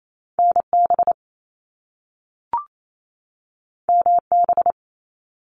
Morse code, heard as a keyed tone, about 28 wpm, 700 Hz. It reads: NB MB
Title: Canadian Provinces and Territories 28wpm